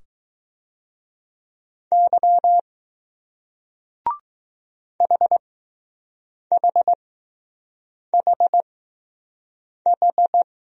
Morse code, heard as a keyed tone, about 20 wpm, spaced as for 6 wpm, 700 Hz. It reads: Y HHHH